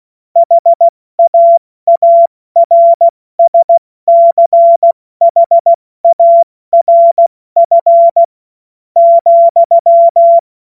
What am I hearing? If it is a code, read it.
Morse code, heard as a keyed tone, about 16 wpm, spaced as for 13 wpm, 680 Hz. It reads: HAARSCHARF,
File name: Morsekurs / Witze / Witz 18. Juli 2023 70 F80